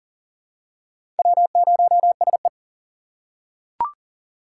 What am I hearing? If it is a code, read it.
Morse code, heard as a keyed tone, about 37 wpm, 700 Hz. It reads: W0SE
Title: USA Call Signs - 4 Characters - 4x Character Spacing - 40wpm